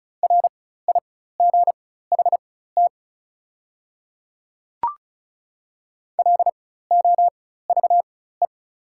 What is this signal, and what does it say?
Morse code, heard as a keyed tone, about 34 wpm, spaced as for 16 wpm, 690 Hz. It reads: RIGHT LOVE
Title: 4x Character Spacing - Top 500 Words - 35wpm